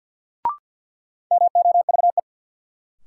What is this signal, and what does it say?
Morse code, heard as a keyed tone, about 53 wpm, 700 Hz. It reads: MOVE
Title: Sentences from Top 200 Words - New Word Drill - 50wpm